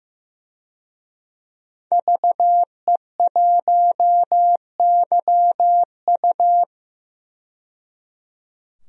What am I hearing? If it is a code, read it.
Morse code, heard as a keyed tone, about 15 wpm, 700 Hz. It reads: VE1YU
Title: DXCC-100 Call Signs - 5 Characters - 15wpm